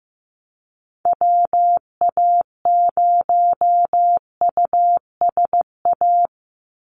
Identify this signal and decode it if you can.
Morse code, heard as a keyed tone, about 15 wpm, 700 Hz. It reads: WA0USA
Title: Call Signs Repeated 3x - 15wpm